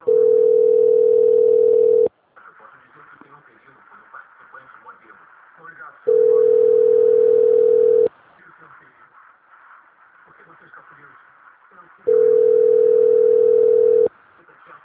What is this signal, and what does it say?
Ringback tone (North America), 440 + 480 Hz, with speech.
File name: out- - - - - .82131